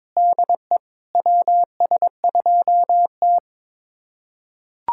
Morse code, heard as a keyed tone, about 22 wpm, 700 Hz. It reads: DE WS2T
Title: USA Callsign DE Callsign - 4 Characters - 22wpm